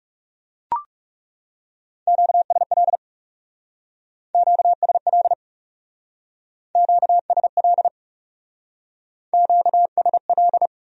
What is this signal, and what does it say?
Morse code, heard as a keyed tone, about 36 wpm, 700 Hz. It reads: QSL QSL QSL QSL
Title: Speed-Racing Q-Codes - 30wpm